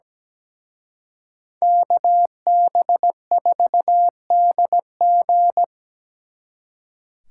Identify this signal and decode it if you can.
Morse code, heard as a keyed tone, about 20 wpm, 700 Hz. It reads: KB4DG
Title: Speed Racing - USA 2x2 Call Signs 17wpm